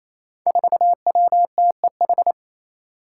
Morse code, heard as a keyed tone, about 28 wpm, 700 Hz. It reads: 4WTEH
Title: Five Challenging Letter-Number Combinations 28wpm